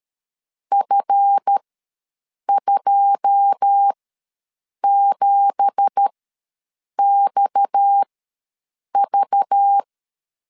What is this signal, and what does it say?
Morse code, heard as a keyed tone, about 13 wpm, spaced as for 7 wpm, 800 Hz. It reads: F27XV
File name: LEZIONI / 9 2867 ws 12wpm 3431